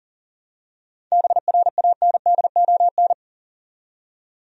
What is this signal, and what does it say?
Morse code, heard as a keyed tone, about 40 wpm, 700 Hz. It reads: BRANDON
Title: USA Common Names - 40wpm